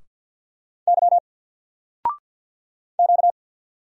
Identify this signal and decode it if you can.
Morse code, heard as a keyed tone, about 50 wpm, 700 Hz. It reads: Y Q
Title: Single Letter-Number 50wpm